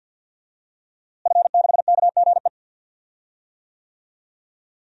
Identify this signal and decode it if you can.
Morse code, heard as a keyed tone, about 46 wpm, 700 Hz. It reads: F6KGE
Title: DXCC-100 Call Signs - 5 Characters - Repeated 2x - 50wpm